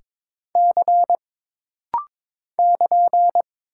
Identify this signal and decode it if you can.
Morse code, heard as a keyed tone, about 22 wpm, 700 Hz. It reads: C <KN>
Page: This rapid-fire timing is geared to push your ICR (Instant Character Recognition) skills.